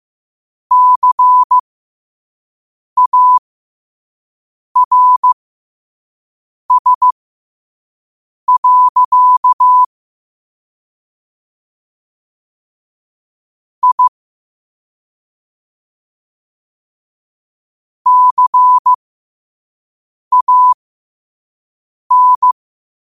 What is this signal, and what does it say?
Morse code, heard as a keyed tone, about 15 wpm, spaced as for 5 wpm, 1000 Hz. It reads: CARS. I CAN
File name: qotdfiles / 5wpm